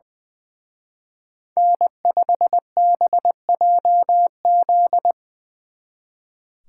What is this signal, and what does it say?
Morse code, heard as a keyed tone, about 21 wpm, 700 Hz. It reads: N5BJZ
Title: Speed Racing - USA 1x3 Call Signs 20wpm